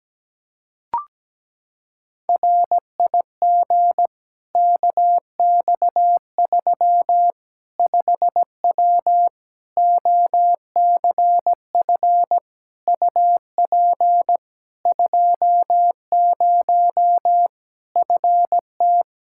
Morse code, heard as a keyed tone, about 17 wpm, 700 Hz. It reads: RIG KX3 5W OCF UP 20 FT
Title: Practice QSO - Exchange 2 - Info - 17wpm